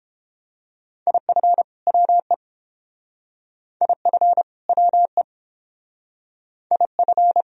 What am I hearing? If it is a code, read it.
Morse code, heard as a keyed tone, about 29 wpm, 700 Hz. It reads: IF WE IF WE IF